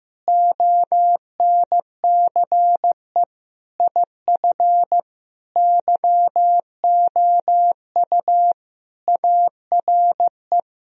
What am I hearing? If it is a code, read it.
Morse code, heard as a keyed tone, about 15 wpm, 700 Hz. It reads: ONCE IF YOU ARE
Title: Sentences from Top 1000 Words - Review Part 2 - 15wpm